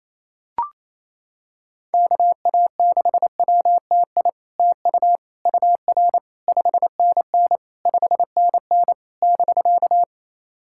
Morse code, heard as a keyed tone, about 28 wpm, 700 Hz. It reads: KA6WTI TU UR 5NN 5NN <BK>
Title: SOTA QSO Practice 28wpm